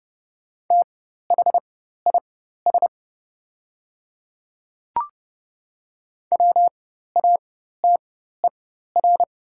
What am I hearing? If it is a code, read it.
Morse code, heard as a keyed tone, about 30 wpm, spaced as for 14 wpm, 700 Hz. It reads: THIS WATER
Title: 4x Character Spacing - Top 100 Words - 30wpm